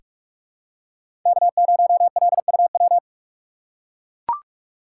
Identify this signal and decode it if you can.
Morse code, heard as a keyed tone, about 42 wpm, 700 Hz. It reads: K0LUW
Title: K1USN SST QSO Practice - 45wpm